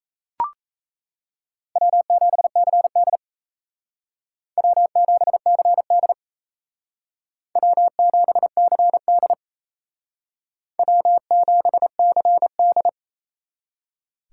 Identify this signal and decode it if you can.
Morse code, heard as a keyed tone, about 34 wpm, 700 Hz. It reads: W7CD W7CD W7CD W7CD
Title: Speed Racing - USA 1x2 Call Signs 28wpm